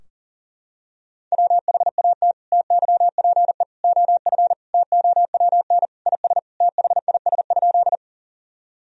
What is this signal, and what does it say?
Morse code, heard as a keyed tone, about 40 wpm, 700 Hz. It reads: WHAT TYPE OF TOWN IS THIS?
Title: Sentences from Top 1000 Words - Review Part 3 - 40wpm